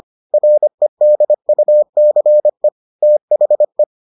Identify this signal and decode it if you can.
Morse code, heard as a keyed tone, about 25 wpm, 600 Hz. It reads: REDUCE THE